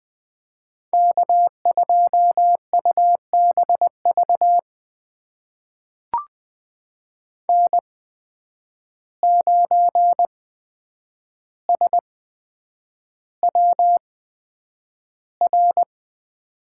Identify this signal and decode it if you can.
Morse code, heard as a keyed tone, about 20 wpm, spaced as for 9 wpm, 700 Hz. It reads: K2UBV N9SWR
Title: USA Call Signs - 5 Characters - 8x Character Spacing - 20wpm